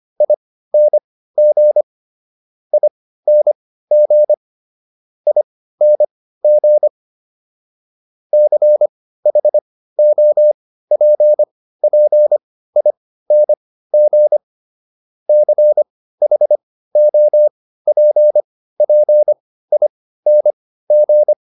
Morse code, heard as a keyed tone, about 25 wpm, spaced as for 15 wpm, 600 Hz. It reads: ING ING ING CHOPPING CHOPPING